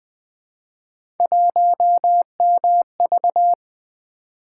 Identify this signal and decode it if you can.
Morse code, heard as a keyed tone, about 20 wpm, 700 Hz. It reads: 1MV
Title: Three Letter-Number Combinations - 20wpm